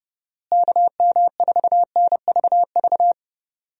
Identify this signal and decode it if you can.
Morse code, heard as a keyed tone, about 31 wpm, 700 Hz. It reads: KM4NVV
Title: USA Call Signs - 6 Characters - Repeated 2x - 30wpm